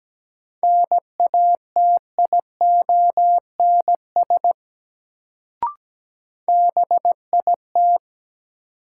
Morse code, heard as a keyed tone, about 17 wpm, 700 Hz. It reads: NATIONS BIT
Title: Sets of 3 Words - New Word Drill - 17wpm